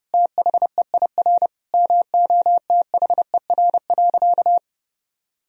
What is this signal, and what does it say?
Morse code, heard as a keyed tone, about 30 wpm, 700 Hz. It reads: THEIR MOTHER.
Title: Sentences from Top 1000 Words - Review Part 3 - 30wpm